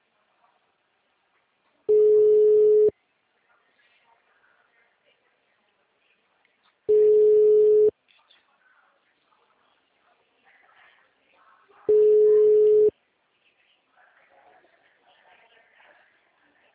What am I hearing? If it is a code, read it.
Ringback tone (Europe, 425 Hz).
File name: out- -2022- - - .309